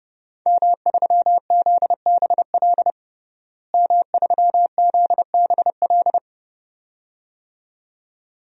Morse code, heard as a keyed tone, about 30 wpm, 700 Hz. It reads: M3ZBL M3ZBL
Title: DXCC-100 Call Signs - 5 Characters - Repeated 2x - 30wpm